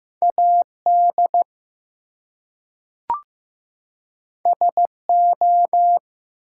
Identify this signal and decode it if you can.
Morse code, heard as a keyed tone, about 15 wpm, 700 Hz. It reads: AD SO
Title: Two Letter Words - 15wpm